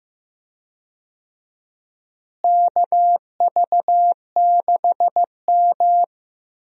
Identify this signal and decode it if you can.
Morse code, heard as a keyed tone, about 15 wpm, 700 Hz. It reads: KV6M
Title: USA Call Signs - 4 Characters - Repeated 2x - 15wpm